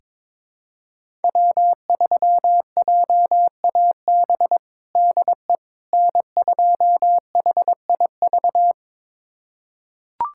Morse code, heard as a keyed tone, about 22 wpm, 700 Hz. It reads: W3JAB DE N2HIV
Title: USA Callsign DE Callsign - 5 Characters - 22wpm